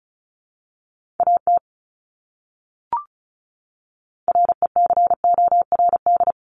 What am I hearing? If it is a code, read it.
Morse code, heard as a keyed tone, about 34 wpm, 700 Hz. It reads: AT RECORD
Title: Top 500 Words Repeated 35wpm